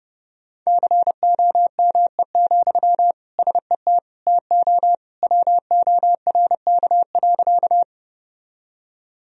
Morse code, heard as a keyed tone, about 30 wpm, 700 Hz. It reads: COME, SET TO WORK.
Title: Sentences from Top 1000 Words - Review Part 1 - 30wpm